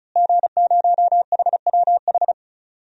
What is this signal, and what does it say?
Morse code, heard as a keyed tone, about 34 wpm, 700 Hz. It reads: G0HWH
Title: DXCC-100 Call Signs - 5 Characters - Repeated 3x - 35wpm